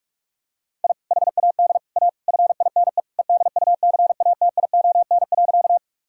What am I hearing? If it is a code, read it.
Morse code, heard as a keyed tone, about 46 wpm, 700 Hz. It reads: I HAD A FINE EDUCATION.
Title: Sentences from Top 1000 Words - Review Part 6 - 45wpm